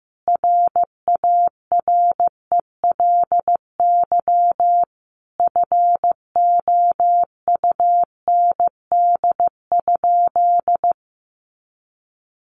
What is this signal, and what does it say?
Morse code, heard as a keyed tone, about 15 wpm, 700 Hz. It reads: RARELY FOUND?
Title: Sentences from Top 1400 Words 15wpm